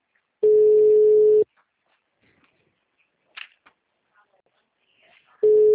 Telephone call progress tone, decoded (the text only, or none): ringback tone